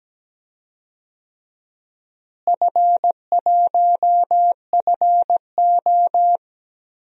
Morse code, heard as a keyed tone, about 17 wpm, 700 Hz. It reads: F1FO
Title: DXCC-100 Call Signs - 4 Characters - Repeated 3x - 17wpm